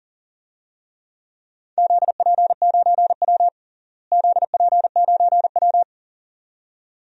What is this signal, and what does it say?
Morse code, heard as a keyed tone, about 40 wpm, 700 Hz. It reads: ZP9W ZP9W
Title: DXCC-100 Call Signs - 4 Characters - Repeated 2x - 40wpm